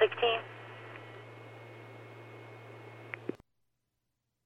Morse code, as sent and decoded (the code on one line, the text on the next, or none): none